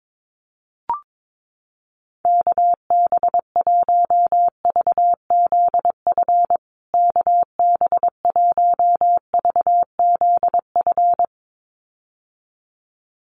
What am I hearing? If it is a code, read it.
Morse code, heard as a keyed tone, about 22 wpm, 700 Hz. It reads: KB1VZF KB1VZF
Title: USA Call Signs Repeated 2x 22wpm